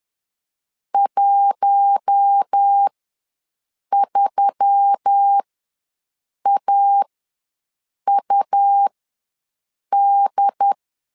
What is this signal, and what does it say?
Morse code, heard as a keyed tone, about 10 wpm, spaced as for 6 wpm, 800 Hz. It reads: 13AUD
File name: LEZIONI / 4 Q1S3 ws 10wpm 2437